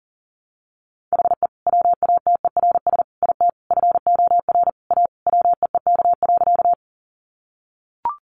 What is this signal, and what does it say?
Morse code, heard as a keyed tone, about 40 wpm, 700 Hz. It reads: HE WATERS IT FOR A WEEK.
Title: Easy Phrases 40wpm